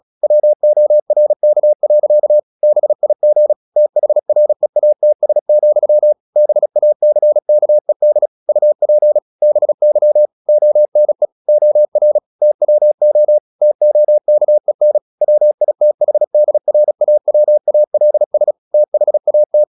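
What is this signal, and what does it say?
Morse code, heard as a keyed tone, about 36 wpm, 600 Hz. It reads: WORK. BIG THREATS, BACKED UP BY ONE OR TWO TOKEN WITHDRAWALS THAT
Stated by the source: Morse practice at 36 wpm, text of Ken Follett's novel Never, from CW med Gnister.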